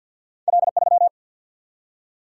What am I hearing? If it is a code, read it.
Morse code, heard as a keyed tone, about 52 wpm, 700 Hz. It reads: L 3
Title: Single-Letter Single-Number 50wpm